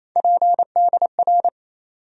Morse code, heard as a keyed tone, about 28 wpm, 700 Hz. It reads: PDR
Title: USA Call Sign Postfixes - 3 Letters 28wpm